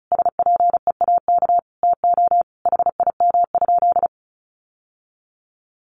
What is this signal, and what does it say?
Morse code, heard as a keyed tone, about 34 wpm, 700 Hz. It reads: SPEAK TO HIM?
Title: Sentences from Top 1100 Words 35wpm